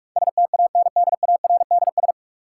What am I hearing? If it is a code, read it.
Morse code, heard as a keyed tone, about 44 wpm, 700 Hz. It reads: STANDARDS
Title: Sentences from Top 1000 Words - Review Part 6 - New Word Drill - 45wpm